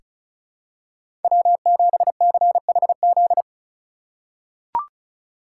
Morse code, heard as a keyed tone, about 36 wpm, 700 Hz. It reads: W7CHZ
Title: USA Call Signs - 5 Characters - 4x Character Spacing - 35wpm